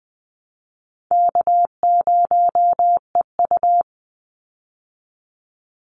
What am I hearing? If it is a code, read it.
Morse code, heard as a keyed tone, about 20 wpm, 700 Hz. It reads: K0EU
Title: Easy Call Signs - 20wpm